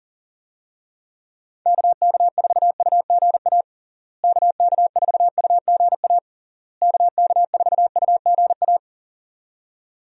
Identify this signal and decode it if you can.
Morse code, heard as a keyed tone, about 37 wpm, 700 Hz. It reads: KK4UGA KK4UGA KK4UGA
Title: USA Call Signs - 6 Characters - Repeated 3x - 40wpm